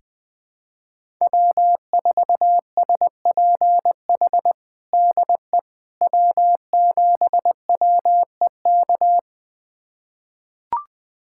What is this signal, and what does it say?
Morse code, heard as a keyed tone, about 20 wpm, 700 Hz. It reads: W4SPH DE W7WEK
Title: USA Callsign DE Callsign - 5 Characters - 20wpm